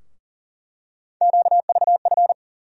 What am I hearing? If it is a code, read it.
Morse code, heard as a keyed tone, about 39 wpm, 700 Hz. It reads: QVF
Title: USA Call Sign Postfixes - 3 Letters 40wpm